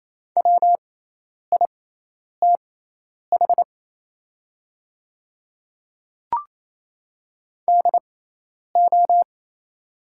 Morse code, heard as a keyed tone, about 27 wpm, spaced as for 10 wpm, 690 Hz. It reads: WITH DO
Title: 6x Character Spacing - Top 100 Words - 28wpm